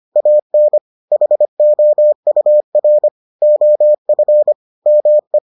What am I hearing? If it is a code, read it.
Morse code, heard as a keyed tone, about 25 wpm, 600 Hz. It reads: AN HOUR OF ME